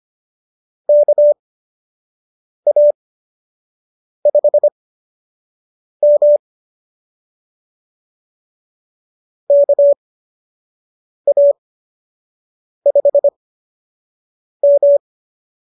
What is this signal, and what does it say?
Morse code, heard as a keyed tone, about 25 wpm, spaced as for 6 wpm, 590 Hz. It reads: KA5M KA5M